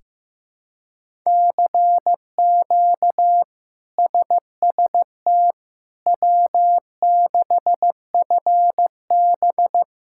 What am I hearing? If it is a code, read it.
Morse code, heard as a keyed tone, about 15 wpm, 700 Hz. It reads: CQ SST W6FB
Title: K1USN SST QSO Practice - 15wpm